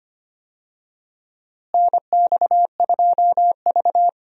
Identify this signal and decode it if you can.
Morse code, heard as a keyed tone, about 25 wpm, 700 Hz. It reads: NX2V
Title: USA Call Signs - 4 Characters - Repeated 3x - 25wpm